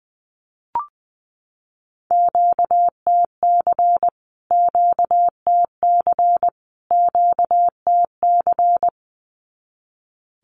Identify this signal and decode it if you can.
Morse code, heard as a keyed tone, about 20 wpm, 700 Hz. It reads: QTC QTC QTC
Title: Q-Codes Repeated 3x 20wpm